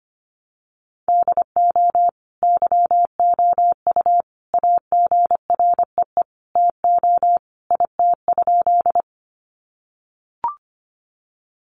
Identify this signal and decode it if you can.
Morse code, heard as a keyed tone, about 25 wpm, 700 Hz. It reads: DO YOU AGREE TO IT?
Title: Sentences from Top 1100 Words 25wpm